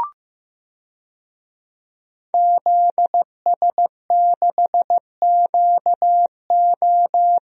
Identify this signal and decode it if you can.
Morse code, heard as a keyed tone, about 15 wpm, 700 Hz. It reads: ZS6QO
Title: DXCC-100 Call Signs - 5 Characters - Repeated 3x - 15wpm